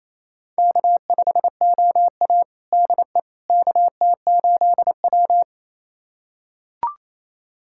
Morse code, heard as a keyed tone, about 28 wpm, 700 Hz. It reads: K5OA DE KT8W
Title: USA Callsign DE Callsign - 4 Characters - 28wpm